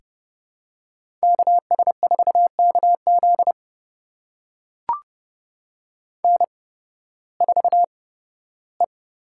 Morse code, heard as a keyed tone, about 30 wpm, spaced as for 16 wpm, 700 Hz. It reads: KS4KZ N4E
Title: USA Call Signs - 8x Character Spacing - 30wpm